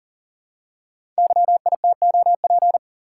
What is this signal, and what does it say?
Morse code, heard as a keyed tone, about 38 wpm, 690 Hz. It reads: YITOP
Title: Code Groups - 5 Letters - 40wpm